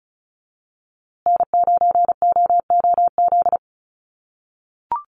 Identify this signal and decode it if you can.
Morse code, heard as a keyed tone, about 34 wpm, 700 Hz. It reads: N9OOZ
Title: USA Call Signs Repeated 3x 35wpm